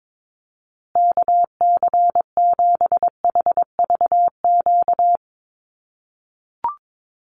Morse code, heard as a keyed tone, about 22 wpm, 700 Hz. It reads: KC7HVQ
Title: USA Call Signs Repeated 3x 22wpm